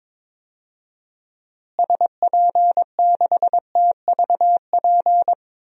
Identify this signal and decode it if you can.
Morse code, heard as a keyed tone, about 22 wpm, 700 Hz. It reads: SP6TVP
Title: DXCC-100 Call Signs - 6 Characters - Repeated 2x - 22wpm